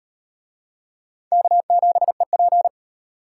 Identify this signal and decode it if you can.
Morse code, heard as a keyed tone, about 38 wpm, 700 Hz. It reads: K7EP